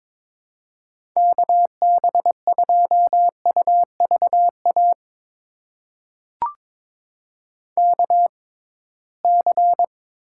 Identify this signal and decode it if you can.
Morse code, heard as a keyed tone, about 22 wpm, spaced as for 17 wpm, 700 Hz. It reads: KB2UVA KC
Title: USA Call Signs - 6 Characters - 6x Character Spacing - 22wpm